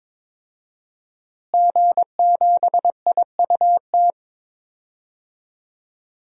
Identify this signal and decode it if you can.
Morse code, heard as a keyed tone, about 22 wpm, 700 Hz. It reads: G7IUT
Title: DXCC-100 Call Signs - 5 Characters - Repeated 2x - 22wpm